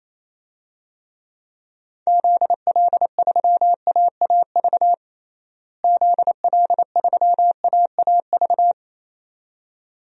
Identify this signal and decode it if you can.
Morse code, heard as a keyed tone, about 27 wpm, 700 Hz. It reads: ZL3AAV ZL3AAV
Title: DXCC-100 Call Signs - 6 Characters - Repeated 2x - 28wpm